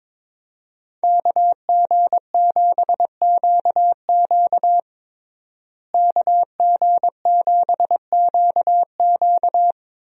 Morse code, heard as a keyed tone, about 21 wpm, 700 Hz. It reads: KG7QQ KG7QQ
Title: USA Call Signs - 5 Characters - Repeated 2x - 22wpm